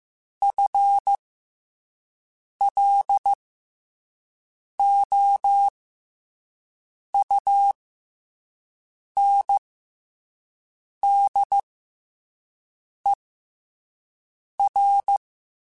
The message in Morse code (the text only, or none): FLOUNDER